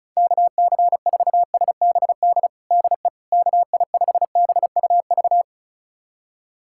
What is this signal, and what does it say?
Morse code, heard as a keyed tone, about 34 wpm, 700 Hz. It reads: KC4SBD DE KI5BUV
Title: USA Callsign DE Callsign - 6 Characters - 35wpm